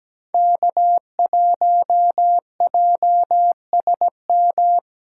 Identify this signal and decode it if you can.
Morse code, heard as a keyed tone, about 17 wpm, 700 Hz. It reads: K1JSM
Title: USA Call Signs - 5 Characters - 6x Character Spacing - 17wpm